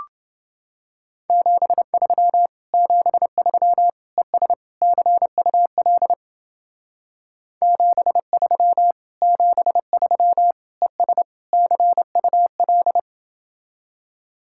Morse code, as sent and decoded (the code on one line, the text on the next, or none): --... ...-- / --... ...-- / . ... / -.-. ..- .-.. / --... ...-- / --... ...-- / . ... / -.-. ..- .-..
73 73 ES CUL 73 73 ES CUL